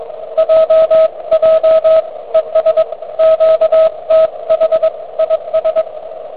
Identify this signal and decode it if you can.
Morse code, heard as a keyed tone, about 22 wpm, 590 Hz, with radio noise. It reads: JJ ES QTH IS